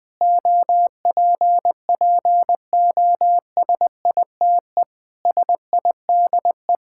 Morse code, heard as a keyed tone, about 20 wpm, 700 Hz. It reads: OPPOSITE SIDE
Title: Sentences with 6-Word Phrases - 20wpm